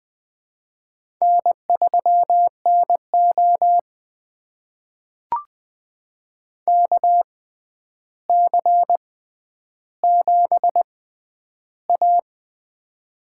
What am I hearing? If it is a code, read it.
Morse code, heard as a keyed tone, about 20 wpm, spaced as for 10 wpm, 700 Hz. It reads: N3NO KC7A
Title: USA Call Signs - 4 Characters - 6x Character Spacing - 20wpm